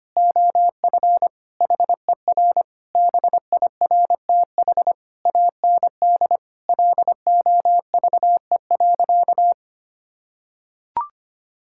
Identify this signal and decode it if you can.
Morse code, heard as a keyed tone, about 25 wpm, 700 Hz. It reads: OF HER BIRTH AND LOVE.
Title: Sentences from Top 1000 Words - Review Part 5 - 25wpm